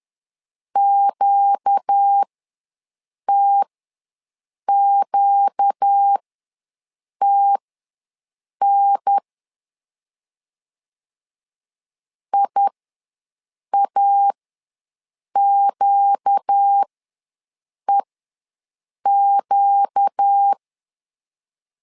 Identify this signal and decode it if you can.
Morse code, heard as a keyed tone, about 10 wpm, spaced as for 6 wpm, 800 Hz. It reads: QTQTN IAQEQ
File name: LEZIONI / 4 Q ws 10wpm 2306